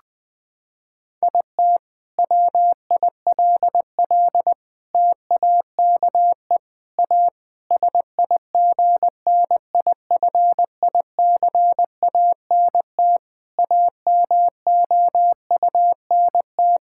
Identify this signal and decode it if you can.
Morse code, heard as a keyed tone, about 20 wpm, 700 Hz. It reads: IT WILL TAKE A SIGNIFICANT AMOUNT
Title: Sentences from Top 1000 Words - Review Part 6 - 20wpm